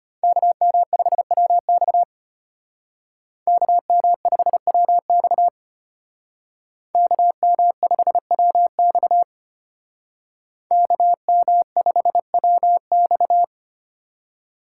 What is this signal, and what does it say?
Morse code, heard as a keyed tone, about 32 wpm, 700 Hz. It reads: KM5WX KM5WX KM5WX KM5WX